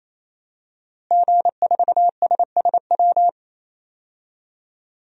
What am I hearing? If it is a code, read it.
Morse code, heard as a keyed tone, about 28 wpm, 700 Hz. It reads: G4SSW
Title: DXCC-100 Call Signs - 5 Characters - Repeated 3x - 28wpm